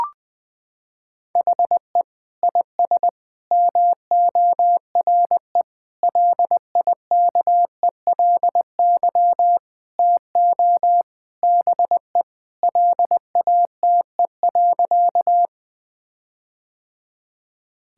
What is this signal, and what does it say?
Morse code, heard as a keyed tone, about 20 wpm, 700 Hz. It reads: HE IS MORE LIKELY TO BE LATE.
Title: Sentences with 4-Word Phrases - 20wpm